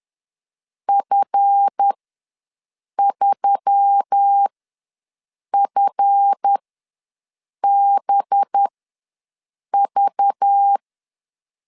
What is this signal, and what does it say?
Morse code, heard as a keyed tone, about 10 wpm, spaced as for 6 wpm, 800 Hz. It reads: F3FBV
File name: LEZIONI / 7 GF9Z / 7 GF ws 10wpm 3918